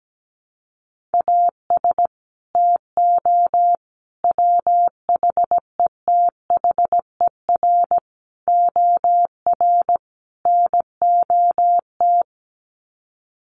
Morse code, heard as a keyed tone, about 17 wpm, 700 Hz. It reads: AS TO WHETHER OR NOT
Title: Sets of 5 Words Encore 17wpm